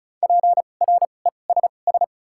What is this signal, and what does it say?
Morse code, heard as a keyed tone, about 36 wpm, 700 Hz. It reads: P R E S S
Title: 2x Character Spacing - Top 300 Words - 35wpm